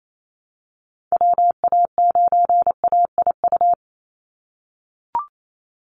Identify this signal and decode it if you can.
Morse code, heard as a keyed tone, about 28 wpm, 700 Hz. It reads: WA9AIU
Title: USA Call Signs 28wpm